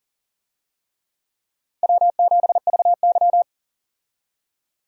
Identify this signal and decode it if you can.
Morse code, heard as a keyed tone, about 40 wpm, 700 Hz. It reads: W7VY
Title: USA Call Signs - 4 Characters - 40wpm